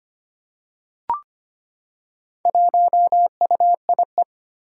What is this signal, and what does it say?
Morse code, heard as a keyed tone, about 25 wpm, 700 Hz. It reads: 1UIE
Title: Four Challenging Letter-Number Combinations 25wpm